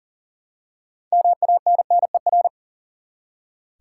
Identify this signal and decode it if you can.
Morse code, heard as a keyed tone, about 40 wpm, 700 Hz. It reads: MANNER
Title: Six Letter Words - 40wpm